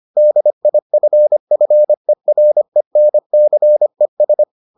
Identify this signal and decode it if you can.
Morse code, heard as a keyed tone, about 25 wpm, 600 Hz. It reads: DIFFERENCES